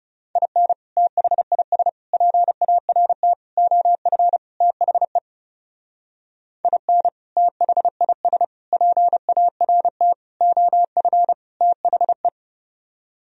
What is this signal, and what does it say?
Morse code, heard as a keyed tone, about 36 wpm, 700 Hz. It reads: IN THIS PART OF THE IN THIS PART OF THE
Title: Speed-Racing - Sets of 5 Words 30wpm